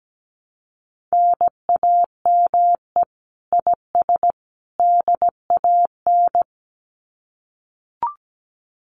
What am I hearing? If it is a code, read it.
Morse code, heard as a keyed tone, about 17 wpm, 700 Hz. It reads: NAME IS DAN
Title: Common QSO Phrases 17wpm